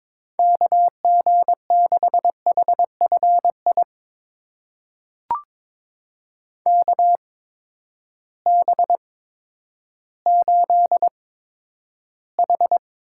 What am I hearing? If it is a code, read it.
Morse code, heard as a keyed tone, about 22 wpm, spaced as for 11 wpm, 700 Hz. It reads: KG6HFI KB8H
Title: USA Call Signs - 6 Characters - 8x Character Spacing - 22wpm